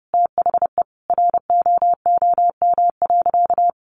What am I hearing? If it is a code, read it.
Morse code, heard as a keyed tone, about 30 wpm, 700 Hz. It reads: THE ROOM.
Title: Sentences from Top 500 Words 30wpm